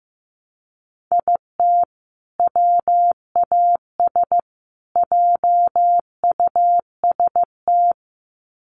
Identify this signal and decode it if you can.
Morse code, heard as a keyed tone, about 15 wpm, 700 Hz. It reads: IT WAS JUST
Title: Sets of 3 Words 15wpm